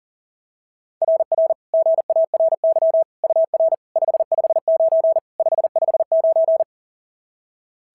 Morse code, heard as a keyed tone, about 37 wpm, 690 Hz. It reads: RR GARY UR 559 559
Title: SOTA QSO Practice 40wpm